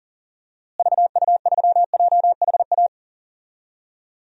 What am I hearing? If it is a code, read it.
Morse code, heard as a keyed tone, about 40 wpm, 700 Hz. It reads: VU3JHA
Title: DXCC-100 Call Signs - 6 Characters - Repeated 2x - 40wpm